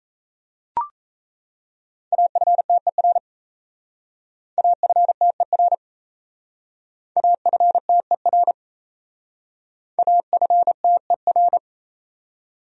Morse code, heard as a keyed tone, about 34 wpm, 700 Hz. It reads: AFTER AFTER AFTER AFTER
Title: Speed-Racing Top 300 Words 28wpm